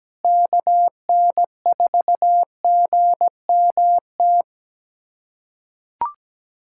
Morse code, heard as a keyed tone, about 17 wpm, 700 Hz. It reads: KN4GMT
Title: USA Callsign DE Callsign - 6 Characters - 17wpm